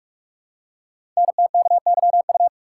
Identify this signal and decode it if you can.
Morse code, heard as a keyed tone, about 44 wpm, 700 Hz. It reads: NTKYU
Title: Code Groups - 5 Letters - 45wpm